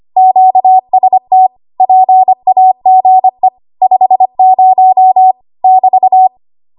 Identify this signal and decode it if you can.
Morse code, heard as a keyed tone, about 25 wpm, 750 Hz. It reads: QST PAGE 50 <BT>
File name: Archive / 25 WPM / 25WPM